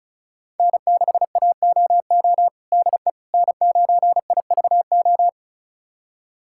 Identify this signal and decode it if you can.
Morse code, heard as a keyed tone, about 36 wpm, 700 Hz. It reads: N6AOO DE N9IVO
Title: USA Callsign DE Callsign - 5 Characters - 35wpm